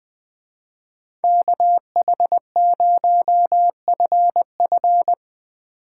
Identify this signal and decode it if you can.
Morse code, heard as a keyed tone, about 20 wpm, 700 Hz. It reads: KH0FF
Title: Speed Racing - USA 2x2 Call Signs 20wpm